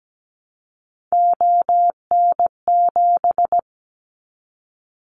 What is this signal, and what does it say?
Morse code, heard as a keyed tone, about 17 wpm, 700 Hz. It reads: ON7
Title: Callsign Prefixes for Top 50 DXCC - 17wpm